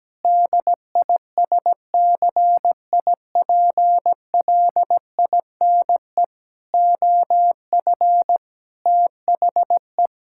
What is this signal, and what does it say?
Morse code, heard as a keyed tone, about 17 wpm, 700 Hz. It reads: DISCIPLINE OF THE
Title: Sentences from Top 1000 Words - Review Part 6 - 17wpm